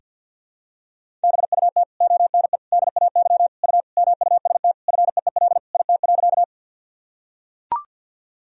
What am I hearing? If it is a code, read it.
Morse code, heard as a keyed tone, about 45 wpm, 700 Hz. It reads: BUT ONE DAY U MUST FEEL IT.
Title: Sentences from Top 400 Words and Abbreviations 50wpm